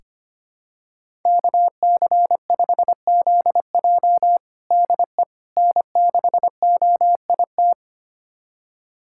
Morse code, heard as a keyed tone, about 25 wpm, 700 Hz. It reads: KC5ZJ DE N6OIT
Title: USA Callsign DE Callsign - 5 Characters - 25wpm